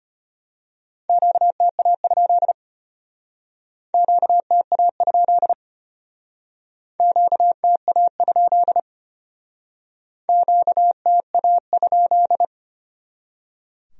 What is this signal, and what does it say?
Morse code, heard as a keyed tone, about 32 wpm, 700 Hz. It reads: QTA? QTA? QTA? QTA?